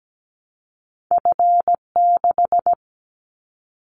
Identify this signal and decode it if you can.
Morse code, heard as a keyed tone, about 17 wpm, 700 Hz. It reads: F6
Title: Callsign Prefixes for Top 50 DXCC - 17wpm